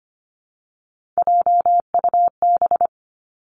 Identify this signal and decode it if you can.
Morse code, heard as a keyed tone, about 25 wpm, 700 Hz. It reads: JUB
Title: Three Challenging Letter-Number Combinations 25wpm